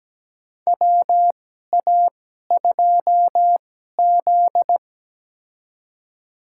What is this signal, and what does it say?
Morse code, heard as a keyed tone, about 17 wpm, spaced as for 12 wpm, 700 Hz. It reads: WA2Z
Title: USA Call Signs - 4 Characters - 2x Character Spacing - 17wpm